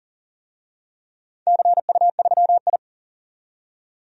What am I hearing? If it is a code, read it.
Morse code, heard as a keyed tone, about 40 wpm, 700 Hz. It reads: CU3I
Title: DXCC-100 Call Signs - 4 Characters - 40wpm